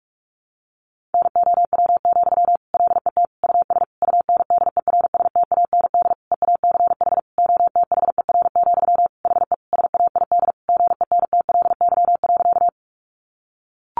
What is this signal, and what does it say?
Morse code, heard as a keyed tone, about 44 wpm, 700 Hz. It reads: NOW, LET US UNDERSTAND EACH OTHER, HE SAID GENTLY.
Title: Sentences from Top 1500 Words 45wpm